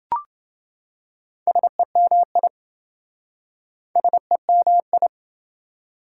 Morse code, heard as a keyed tone, about 28 wpm, 700 Hz. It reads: SEMI SEMI